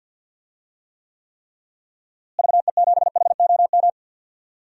Morse code, heard as a keyed tone, about 51 wpm, 700 Hz. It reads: VE7HOM
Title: DXCC-100 Call Signs - 6 Characters - Repeated 3x - 50wpm